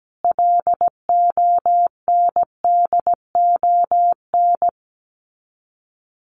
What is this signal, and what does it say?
Morse code, heard as a keyed tone, about 17 wpm, 700 Hz. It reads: LONDON
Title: Common QSO Phrases 17wpm